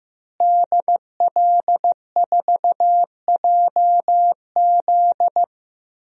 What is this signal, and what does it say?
Morse code, heard as a keyed tone, about 15 wpm, 700 Hz. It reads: DL4JZ
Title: DXCC-100 Call Signs - 5 Characters - Repeated 3x - 15wpm